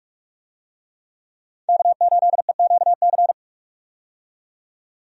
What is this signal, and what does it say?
Morse code, heard as a keyed tone, about 46 wpm, 700 Hz. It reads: K8EQC
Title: USA Call Signs - 5 Characters - 45wpm